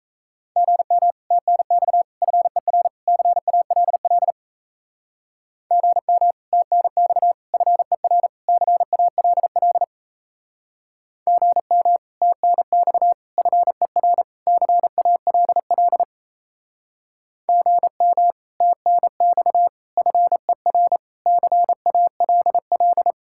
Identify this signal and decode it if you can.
Morse code, heard as a keyed tone, about 36 wpm, 700 Hz. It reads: GM TNX FER CALL GM TNX FER CALL GM TNX FER CALL GM TNX FER CALL